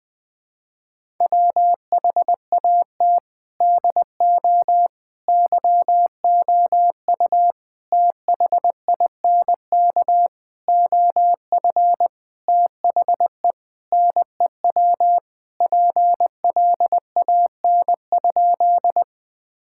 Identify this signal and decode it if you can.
Morse code, heard as a keyed tone, about 20 wpm, 700 Hz. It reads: WHAT DO YOU THINK OF THE NEW PLAN?
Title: Sentences with 6-Word Phrases - 20wpm